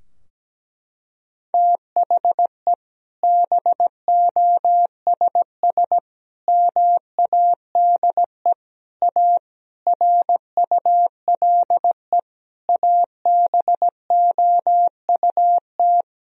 Morse code, heard as a keyed tone, about 17 wpm, 700 Hz. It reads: THE BOSS MADE A RULE ABOUT
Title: Sentences from Top 1000 Words - Review Part 4 - 17wpm